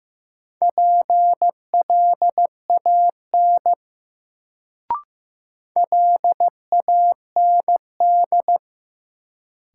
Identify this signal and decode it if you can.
Morse code, heard as a keyed tone, about 15 wpm, 700 Hz. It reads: PLAN LAND
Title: Sentences from Top 500 Words - New Word Drill - 15wpm